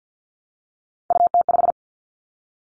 Morse code, heard as a keyed tone, about 50 wpm, 700 Hz. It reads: UT5
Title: Callsign Prefixes for Top 25 DXCC - Repeated 3x - 50wpm